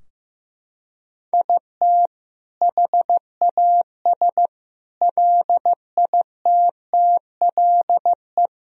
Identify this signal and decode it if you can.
Morse code, heard as a keyed tone, about 16 wpm, 700 Hz. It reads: IT HAS LITTLE
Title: Sentences from Top 1000 Words - Review Part 4 - 15wpm